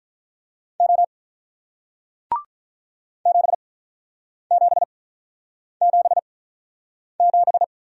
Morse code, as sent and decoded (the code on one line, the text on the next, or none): --- / --... --... --... --...
O 7777